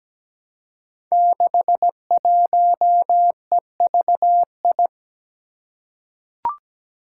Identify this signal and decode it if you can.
Morse code, heard as a keyed tone, about 17 wpm, 700 Hz. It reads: 61EVI
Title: Five Challenging Letter-Number Combinations 17wpm